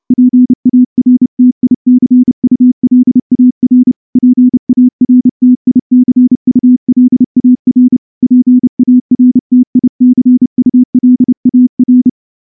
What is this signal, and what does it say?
Morse code, heard as a keyed tone, about 29 wpm, 250 Hz. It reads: PARTICULAR PARTICULAR PARTICULAR
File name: morse-training-30WPM